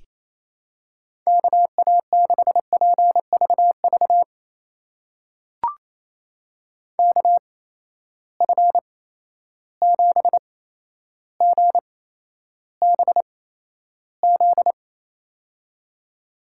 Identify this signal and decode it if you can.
Morse code, heard as a keyed tone, about 28 wpm, spaced as for 12 wpm, 700 Hz. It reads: KA6PVV KF7GBZ
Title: USA Call Signs - 6 Characters - 8x Character Spacing - 28wpm